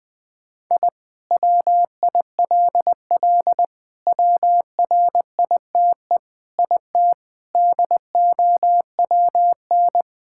Morse code, heard as a keyed tone, about 21 wpm, 700 Hz. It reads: I WILL WRITE IT DOWN
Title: Sentences with 4-Word Phrases - Spoken Beginning - 20wpm